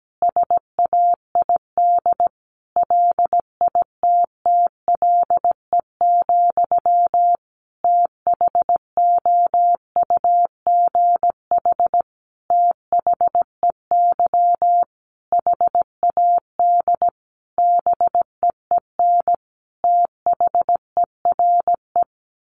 Morse code, heard as a keyed tone, about 17 wpm, 700 Hz. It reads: SAID LITTLE, THOUGH THEY HAD BEEN THERE
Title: Sentences from Top 400 Words 17wpm